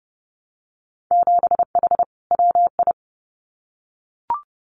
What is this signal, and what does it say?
Morse code, heard as a keyed tone, about 32 wpm, 700 Hz. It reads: 7H WI
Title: Winter Field Day Exchanges Encore 30wpm